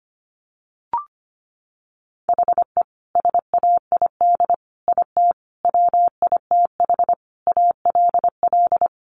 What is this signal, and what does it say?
Morse code, heard as a keyed tone, about 25 wpm, 700 Hz. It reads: HE SAID IT WITH ALL
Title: Sentences from Top 1200 Words 25wpm